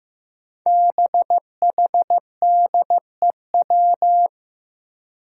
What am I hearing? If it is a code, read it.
Morse code, heard as a keyed tone, about 15 wpm, 700 Hz. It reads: BHDEW
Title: Five Challenging Letter-Number Combinations 15wpm